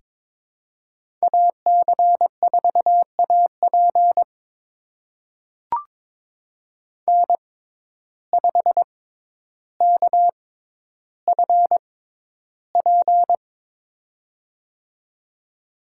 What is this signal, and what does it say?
Morse code, heard as a keyed tone, about 23 wpm, spaced as for 11 wpm, 700 Hz. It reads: AC4AP N5KFP
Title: USA Call Signs - 5 Characters - 6x Character Spacing - 22wpm